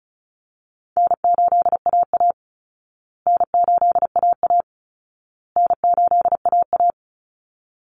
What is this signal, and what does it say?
Morse code, heard as a keyed tone, about 34 wpm, 700 Hz. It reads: N8AA N8AA N8AA
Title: Easy Call Signs Repeated 3x - 35wpm